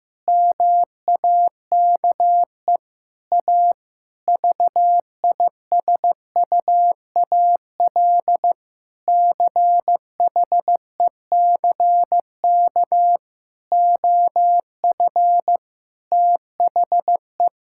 Morse code, heard as a keyed tone, about 15 wpm, 700 Hz. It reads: MAKE A VISUAL CHECK OF THE
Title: Sentences from Top 1000 Words - Review Part 6 - 15wpm